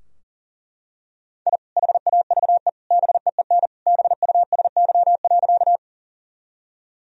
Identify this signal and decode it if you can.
Morse code, heard as a keyed tone, about 40 wpm, 700 Hz. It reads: I HAVE BEEN BUSY.
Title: Sentences from Top 1000 Words - Review Part 3 - 40wpm